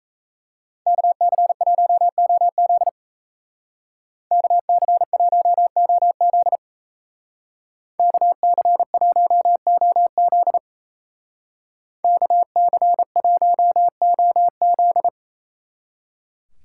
Morse code, heard as a keyed tone, about 37 wpm, 700 Hz. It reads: KC1OZ KC1OZ KC1OZ KC1OZ